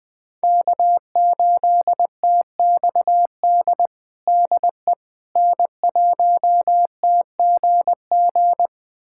Morse code, heard as a keyed tone, about 21 wpm, 700 Hz. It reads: K8TXD DE N1TGG
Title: USA Callsign DE Callsign - 5 Characters - 20wpm